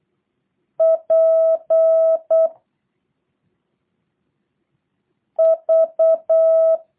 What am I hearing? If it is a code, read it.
Morse code, heard as a keyed tone, about 8 wpm, 660 Hz. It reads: P V
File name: jh8cbh / 2023cw / review1-5